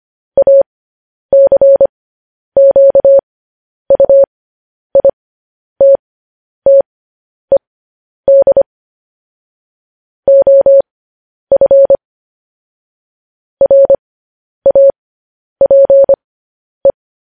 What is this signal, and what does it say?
Morse code, heard as a keyed tone, about 25 wpm, spaced as for 10 wpm, 550 Hz. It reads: ACQUITTED OF RAPE
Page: news-headlines-in-morse-code-at-10-wpm